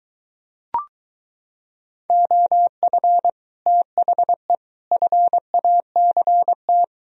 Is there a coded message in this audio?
Yes: Morse code from 2.1 to 6.8 s, heard as a keyed tone, about 23 wpm, 700 Hz. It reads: OF THE FACT